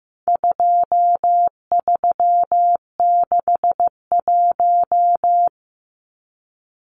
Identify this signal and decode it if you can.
Morse code, heard as a keyed tone, about 15 wpm, 700 Hz. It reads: 2361
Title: Numbers - 4-Digit 15wpm